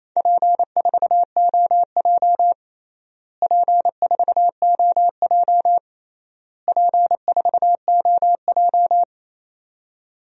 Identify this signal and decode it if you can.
Morse code, heard as a keyed tone, about 28 wpm, 700 Hz. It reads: P4OJ P4OJ P4OJ
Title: DXCC-100 Call Signs - 4 Characters - Repeated 3x - 28wpm